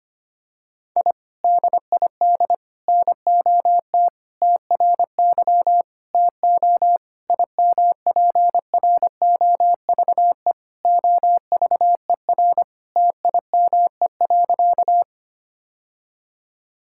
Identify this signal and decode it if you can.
Morse code, heard as a keyed tone, about 25 wpm, 700 Hz. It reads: I DID NOT TRY TO IMPROVE OVER TIME.
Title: Sentences from Top 1000 Words - Review Part 5 - 25wpm